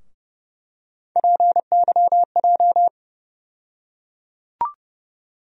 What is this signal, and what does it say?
Morse code, heard as a keyed tone, about 30 wpm, 700 Hz. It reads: PYJ
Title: USA Call Sign Postfixes - 3 Letters 30wpm